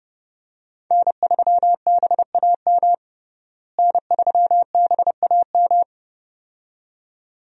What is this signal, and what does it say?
Morse code, heard as a keyed tone, about 30 wpm, 700 Hz. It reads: N3BAM N3BAM
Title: USA Call Signs - 5 Characters - Repeated 2x - 30wpm